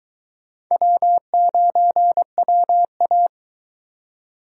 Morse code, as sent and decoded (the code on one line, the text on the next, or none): .-- ----. .-- .-
W9WA